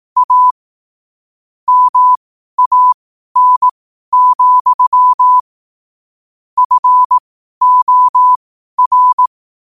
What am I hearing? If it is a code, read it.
Morse code, heard as a keyed tone, about 18 wpm, spaced as for 12 wpm, 1000 Hz. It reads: A MAN, FOR
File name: qotdfiles / 12wpm